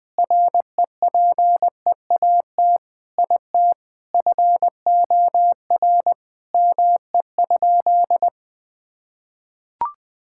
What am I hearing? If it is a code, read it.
Morse code, heard as a keyed tone, about 21 wpm, 700 Hz. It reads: REPEAT IT FOR ME?
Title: Sentences from Top 1600 Words 20wpm